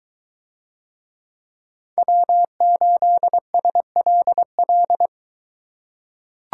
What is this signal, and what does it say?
Morse code, heard as a keyed tone, about 22 wpm, 700 Hz. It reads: W8SLL